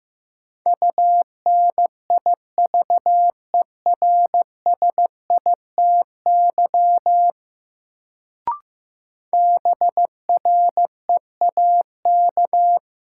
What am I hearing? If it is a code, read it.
Morse code, heard as a keyed tone, about 15 wpm, 700 Hz. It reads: UNIVERSITY BREAK
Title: Sets of 4 Words - New Word Drill - 15wpm